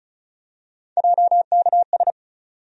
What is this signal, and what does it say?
Morse code, heard as a keyed tone, about 36 wpm, 700 Hz. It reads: JKS
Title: USA Call Sign Postfixes - 3 Letters 35wpm